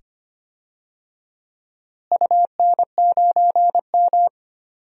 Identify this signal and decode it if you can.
Morse code, heard as a keyed tone, about 25 wpm, 700 Hz. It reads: UN9M
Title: DXCC-100 Call Signs - 4 Characters - Repeated 2x - 25wpm